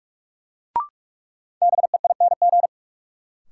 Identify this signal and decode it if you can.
Morse code, heard as a keyed tone, about 46 wpm, 700 Hz. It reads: BEING
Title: Sets of 4 Words - New Word Drill - 45wpm